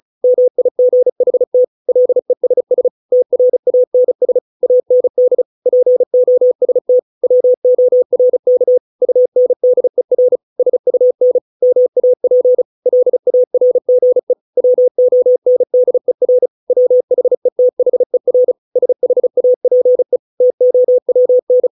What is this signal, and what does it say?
Morse code, heard as a keyed tone, about 34 wpm, 500 Hz. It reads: MIGHT LESS TRANS AND POST WORK UNDER SUN MAP LARGE WONDER WHETHER SHAPE TOWN